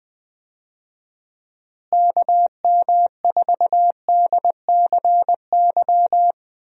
Morse code, heard as a keyed tone, about 20 wpm, 700 Hz. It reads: KM4DCY
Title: USA Call Signs - 6 Characters - Repeated 2x - 20wpm